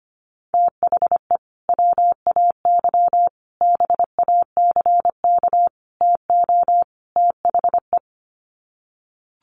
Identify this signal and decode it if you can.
Morse code, heard as a keyed tone, about 25 wpm, 700 Hz. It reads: THE WAY BACK TO THE
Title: Sets of 5 Words Encore2 25wpm